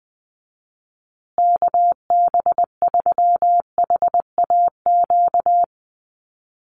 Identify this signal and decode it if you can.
Morse code, heard as a keyed tone, about 20 wpm, 700 Hz. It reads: KB3HAQ
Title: USA Call Signs Repeated 2x 20wpm